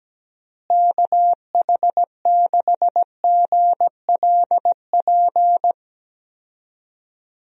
Morse code, heard as a keyed tone, about 17 wpm, 700 Hz. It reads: KH6GLP
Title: DXCC-100 Call Signs - 6 Characters - Repeated 2x - 17wpm